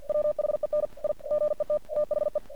Morse code, heard as a keyed tone, about 48 wpm, 600 Hz. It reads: WHEN I GET THE
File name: archived / CRSnet